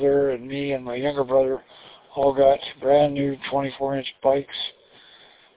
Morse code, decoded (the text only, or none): none